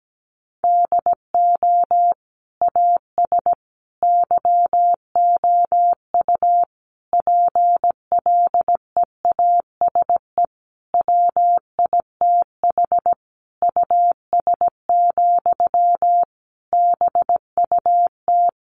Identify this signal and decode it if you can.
Morse code, heard as a keyed tone, about 17 wpm, 700 Hz. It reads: DO AS YOU PLEASE WITH US, BUT
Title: Sentences from Top 1500 Words 17wpm